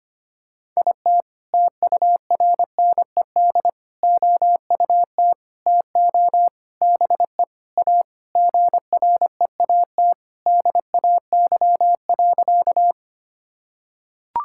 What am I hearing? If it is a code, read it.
Morse code, heard as a keyed tone, about 25 wpm, 700 Hz. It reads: IT TURNED OUT TO BE A GREAT DAY.
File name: practice / 25wpm / Sentences with 4-Word Phrases - 25wpm